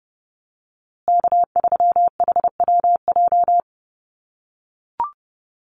Morse code, heard as a keyed tone, about 30 wpm, 700 Hz. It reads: K3HWJ
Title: USA Call Signs Repeated 3x 30wpm